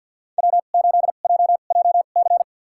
Morse code, heard as a keyed tone, about 49 wpm, spaced as for 36 wpm, 690 Hz. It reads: W8JJC
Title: USA Call Signs - 5 Characters - 2x Character Spacing - 50wpm